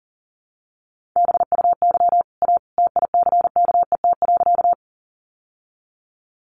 Morse code, heard as a keyed tone, about 40 wpm, 700 Hz. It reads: BUY A TICKET.
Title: Sentences from Top 1400 Words 40wpm